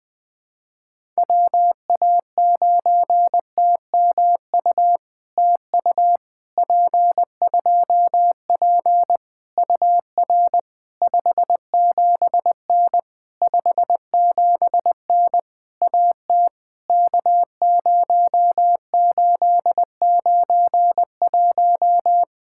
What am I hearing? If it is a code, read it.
Morse code, heard as a keyed tone, about 20 wpm, 700 Hz. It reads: WA9TMU TU P2P UR 57N 57N AT K0891
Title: POTA QSO Practice - 20wpm